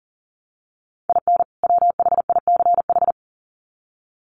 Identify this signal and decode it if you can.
Morse code, heard as a keyed tone, about 40 wpm, 700 Hz. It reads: IN WHICH
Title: Sets of 2 Words 40wpm